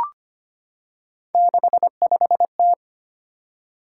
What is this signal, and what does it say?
Morse code, heard as a keyed tone, about 25 wpm, 700 Hz. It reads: 65T
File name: Three Letter-Number Combinations - 25wpm